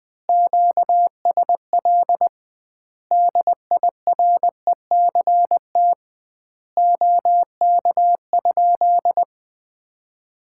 Morse code, heard as a keyed tone, about 21 wpm, 700 Hz. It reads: QSL DIRECT OK?
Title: Common QSO Phrases - 2x Word-Spacing - 20wpm